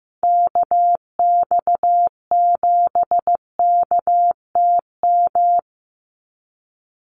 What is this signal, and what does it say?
Morse code, heard as a keyed tone, about 15 wpm, 700 Hz. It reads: KX7KTM
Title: USA Call Signs 15wpm